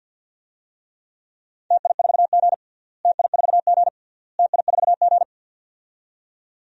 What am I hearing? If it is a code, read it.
Morse code, heard as a keyed tone, about 51 wpm, 700 Hz. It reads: TI4G TI4G TI4G
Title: DXCC-100 Call Signs - 4 Characters - Repeated 3x - 50wpm